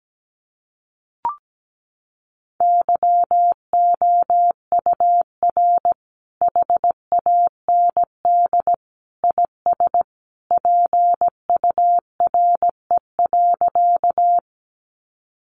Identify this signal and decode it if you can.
Morse code, heard as a keyed tone, about 17 wpm, 700 Hz. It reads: YOUR HAND IS PURE.
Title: Sentences from Top 1500 Words 17wpm